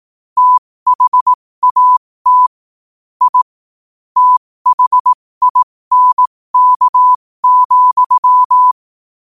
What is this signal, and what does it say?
Morse code, heard as a keyed tone, about 18 wpm, spaced as for 15 wpm, 1000 Hz. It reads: THAT I THINK,
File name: qotdfiles / 15wpm